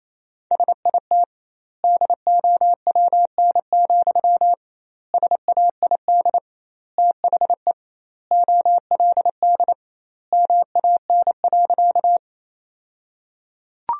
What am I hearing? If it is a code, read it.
Morse code, heard as a keyed tone, about 28 wpm, 700 Hz. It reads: SIT DOWN, SAID THE OLD MAN.
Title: 2x Word-Spacing - Sentences from Top 500 Words - 28wpm